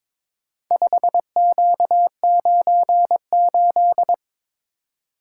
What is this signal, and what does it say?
Morse code, heard as a keyed tone, about 22 wpm, 700 Hz. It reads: 5Q98
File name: practice / 22wpm / Four Letter-Number Combinations - 22wpm